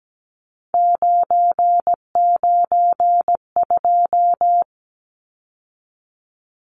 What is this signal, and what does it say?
Morse code, heard as a keyed tone, about 17 wpm, 700 Hz. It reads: 992
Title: Numbers - 3-Digit 17wpm